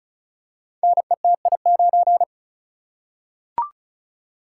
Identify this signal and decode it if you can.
Morse code, heard as a keyed tone, about 36 wpm, 700 Hz. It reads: NETI9
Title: Code Groups - 5 Letter:Numbers - 35wpm